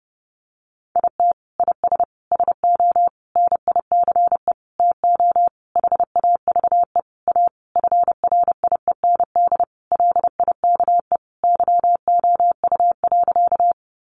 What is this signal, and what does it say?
Morse code, heard as a keyed tone, about 30 wpm, 700 Hz. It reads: IT IS SO NICE TO HAVE A FRIEND LIKE YOU.
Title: Sentences from Top 1100 Words 30wpm